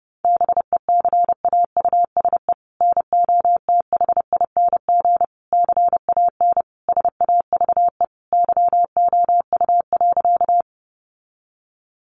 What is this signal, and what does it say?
Morse code, heard as a keyed tone, about 28 wpm, 700 Hz. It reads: BECAUSE NOTHING CAN SAVE YOU.
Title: Sentences from Top 1100 Words 30wpm